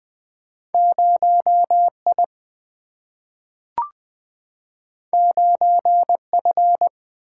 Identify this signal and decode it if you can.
Morse code, heard as a keyed tone, about 20 wpm, 700 Hz. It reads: 0 I 9 F
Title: Single-Letter Single-Number 20wpm